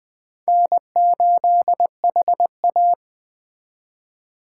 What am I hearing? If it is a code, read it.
Morse code, heard as a keyed tone, about 23 wpm, 700 Hz. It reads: N8HA
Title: Speed Racing - USA 1x2 Call Signs 20wpm